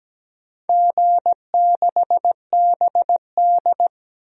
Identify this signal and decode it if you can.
Morse code, heard as a keyed tone, about 17 wpm, 700 Hz. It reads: G6BD
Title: DXCC-100 Call Signs - 4 Characters - 17wpm